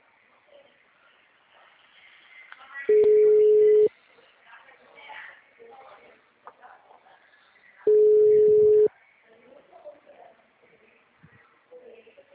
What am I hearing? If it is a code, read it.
Ringback tone (Europe, 425 Hz).
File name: out- -2002- - - .26464